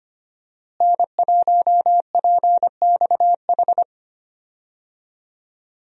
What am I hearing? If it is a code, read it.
Morse code, heard as a keyed tone, about 25 wpm, 700 Hz. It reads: N1PXH
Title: USA Call Signs - 5 Characters - Repeated 3x - 25wpm